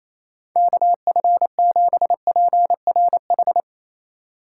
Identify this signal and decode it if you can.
Morse code, heard as a keyed tone, about 28 wpm, 700 Hz. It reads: KF7PRH
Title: USA Call Signs - 6 Characters - 4x Character Spacing - 28wpm